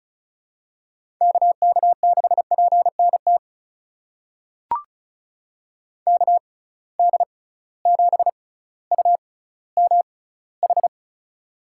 Morse code, heard as a keyed tone, about 34 wpm, spaced as for 18 wpm, 700 Hz. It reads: KK6PNT KD7UMH
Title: USA Call Signs - 6 Characters - 6x Character Spacing - 35wpm